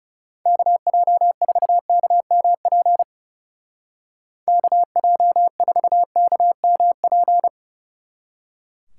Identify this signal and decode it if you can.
Morse code, heard as a keyed tone, about 36 wpm, 700 Hz. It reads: KJ4KMP KJ4KMP
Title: Speed Racing - USA 2x3 Call Signs 30wpm